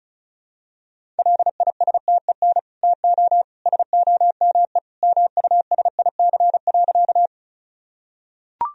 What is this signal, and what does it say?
Morse code, heard as a keyed tone, about 34 wpm, 700 Hz. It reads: LISTEN TO SOME MUSIC.
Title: Sentences from Top 1000 Words - Review Part 1 - 35wpm